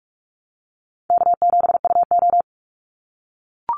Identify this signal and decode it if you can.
Morse code, heard as a keyed tone, about 44 wpm, 700 Hz. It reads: K7UO
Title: USA Call Signs 45wpm